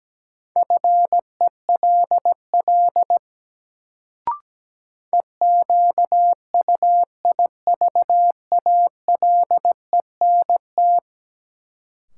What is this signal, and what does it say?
Morse code, heard as a keyed tone, about 17 wpm, 700 Hz. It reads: FELL EQUIVALENT
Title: Sentences from Top 1600 Words - New Word Drill Alt - 17wpm